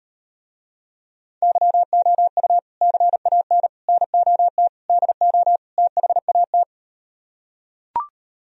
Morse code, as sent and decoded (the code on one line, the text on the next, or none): -.-- --- ..- / -.-. .- -. / -. --- - / -.. --- / - .... .- -
YOU CAN NOT DO THAT